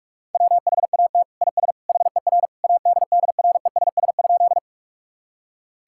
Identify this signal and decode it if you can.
Morse code, heard as a keyed tone, about 46 wpm, 700 Hz. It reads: WHAT IS HER ADDRESS?
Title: Sentences from Top 1000 Words - Review Part 6 - 45wpm